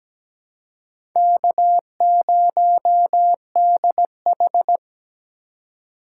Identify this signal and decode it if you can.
Morse code, heard as a keyed tone, about 17 wpm, 700 Hz. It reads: K0DH
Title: USA Call Signs - 4 Characters - 2x Character Spacing - 17wpm